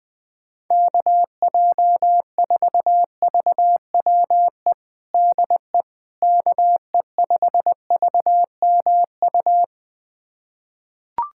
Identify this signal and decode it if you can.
Morse code, heard as a keyed tone, about 21 wpm, 700 Hz. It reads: KJ4VWE DE KE5VMU
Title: USA Callsign DE Callsign - 6 Characters - 20wpm